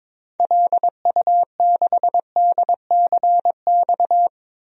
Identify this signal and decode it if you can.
Morse code, heard as a keyed tone, about 22 wpm, 700 Hz. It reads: LU6DCX
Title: DXCC-100 Call Signs - 6 Characters - 22wpm